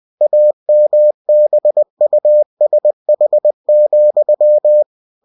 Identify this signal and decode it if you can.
Morse code, heard as a keyed tone, about 20 wpm, 600 Hz. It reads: AMBUSH,
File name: cw-books / winnie-the-pooh / winnie-20wpm-page-0082